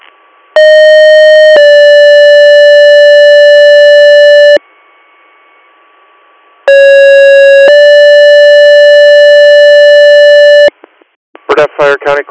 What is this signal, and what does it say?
Two-tone page, heard as a one-second tone, then a three-second tone, about 620 then 580 Hz, about 550 then 580 Hz, with speech. It is heard from 0.6 to 10.7 s.